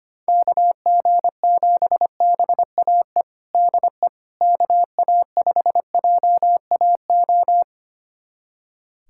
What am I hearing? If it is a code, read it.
Morse code, heard as a keyed tone, about 25 wpm, 700 Hz. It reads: KG7BAE DE KA5JAO
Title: USA Callsign DE Callsign - 6 Characters - 25wpm